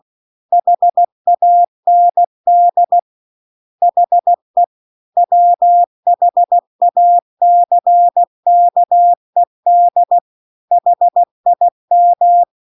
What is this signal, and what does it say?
Morse code, heard as a keyed tone, about 16 wpm, 700 Hz. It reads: HAND HE WHACKED HIM
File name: CW-The-Big-Twoheaded-River-2-16WPM